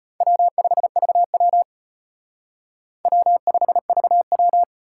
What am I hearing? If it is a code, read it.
Morse code, heard as a keyed tone, about 32 wpm, 700 Hz. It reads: W5VW W5VW